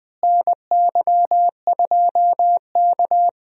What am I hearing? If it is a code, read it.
Morse code, heard as a keyed tone, about 20 wpm, 700 Hz. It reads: NY2K
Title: USA Call Signs - 4 Characters - Repeated 3x - 20wpm